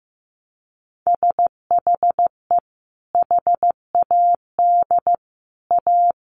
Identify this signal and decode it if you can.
Morse code, heard as a keyed tone, about 15 wpm, 700 Hz. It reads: SHE HAD A
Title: Sentences from Top 600 Words 15wpm